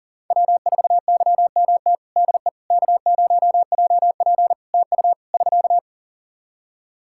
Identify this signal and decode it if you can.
Morse code, heard as a keyed tone, about 40 wpm, 700 Hz. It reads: W4YMT DE K0JP TU <SK>
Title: Practice QSO - Exchange 4 - Ending - 40wpm